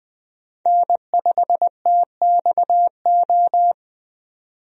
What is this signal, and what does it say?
Morse code, heard as a keyed tone, about 20 wpm, 700 Hz. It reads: N5TXO
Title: USA Call Signs - 5 Characters - 20wpm